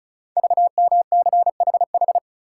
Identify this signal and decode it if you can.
Morse code, heard as a keyed tone, about 36 wpm, 700 Hz. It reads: VMCHH
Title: Code Groups - 5 Letter:Numbers - 35wpm